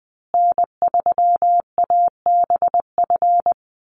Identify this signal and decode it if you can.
Morse code, heard as a keyed tone, about 20 wpm, 700 Hz. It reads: N3ABF
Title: USA Call Signs Repeated 2x 20wpm